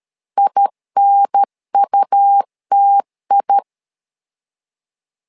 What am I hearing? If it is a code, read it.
Morse code, heard as a keyed tone, about 13 wpm, 800 Hz. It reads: INUTI
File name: LEZIONI / 2 UHDN / 2 UHDN 12wpm 1456